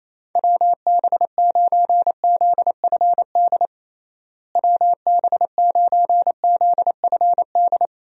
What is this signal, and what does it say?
Morse code, heard as a keyed tone, about 28 wpm, 700 Hz. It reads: WB9ZFD WB9ZFD
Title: USA Call Signs - 6 Characters - Repeated 2x - 28wpm